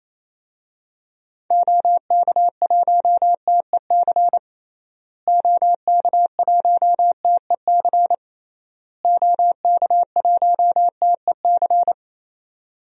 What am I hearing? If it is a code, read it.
Morse code, heard as a keyed tone, about 28 wpm, 700 Hz. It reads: OK1TEC OK1TEC OK1TEC
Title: DXCC-100 Call Signs - 6 Characters - Repeated 3x - 28wpm